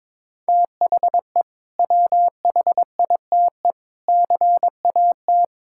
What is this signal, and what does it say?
Morse code, heard as a keyed tone, about 22 wpm, 700 Hz. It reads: THE WHITE CAT
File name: practice / 22wpm / Sentences with 2-Word Phrases - 22wpm